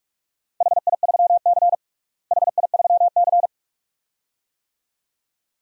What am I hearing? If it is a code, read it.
Morse code, heard as a keyed tone, about 41 wpm, 700 Hz. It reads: HI3C HI3C
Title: DXCC-100 Call Signs - 4 Characters - Repeated 2x - 45wpm